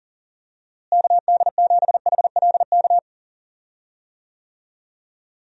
Morse code, heard as a keyed tone, about 40 wpm, 700 Hz. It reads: KD7HLK
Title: USA Call Signs - 6 Characters - Repeated 2x - 40wpm